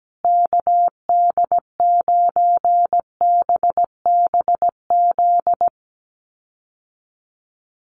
Morse code, heard as a keyed tone, about 17 wpm, 700 Hz. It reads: KD9BBZ
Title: USA Call Signs Repeated 3x 17wpm